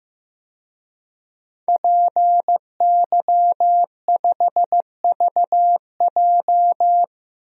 Morse code, heard as a keyed tone, about 15 wpm, 700 Hz. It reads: PY5VJ
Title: DXCC-100 Call Signs - 5 Characters - Repeated 3x - 15wpm